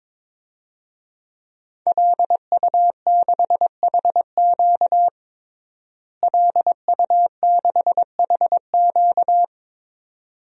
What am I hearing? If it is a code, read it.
Morse code, heard as a keyed tone, about 22 wpm, 700 Hz. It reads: LU6HQ LU6HQ
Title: DXCC-100 Call Signs - 5 Characters - Repeated 2x - 22wpm